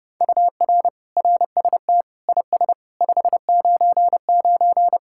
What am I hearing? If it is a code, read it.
Morse code, heard as a keyed tone, about 29 wpm, 700 Hz. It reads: UR RST IS 599